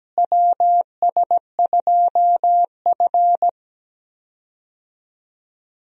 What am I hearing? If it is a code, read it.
Morse code, heard as a keyed tone, about 17 wpm, 700 Hz. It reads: WS2F
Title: USA Call Signs - 4 Characters - Repeated 3x - 17wpm